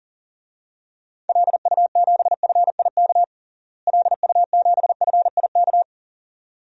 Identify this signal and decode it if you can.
Morse code, heard as a keyed tone, about 40 wpm, 700 Hz. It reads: LU7FIK LU7FIK
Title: DXCC-100 Call Signs - 6 Characters - Repeated 2x - 40wpm